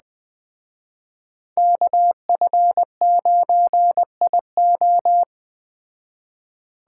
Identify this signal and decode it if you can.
Morse code, heard as a keyed tone, about 20 wpm, 700 Hz. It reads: KF9IO
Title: Speed Racing - USA 2x2 Call Signs 17wpm